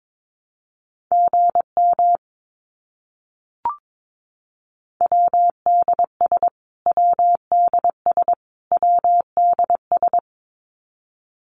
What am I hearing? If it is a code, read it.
Morse code, heard as a keyed tone, about 22 wpm, 700 Hz. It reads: GM WDS WDS WDS
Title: Abbreviations Repeated 3x - 22wpm